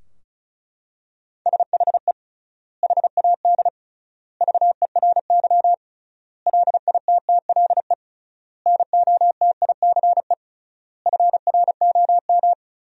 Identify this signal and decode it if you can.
Morse code, heard as a keyed tone, about 36 wpm, 700 Hz. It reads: SHE HAD VERY LITTLE NOTICE FROM
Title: 3x Word-Spacing - Sentences from Top 400 Words - 35wpm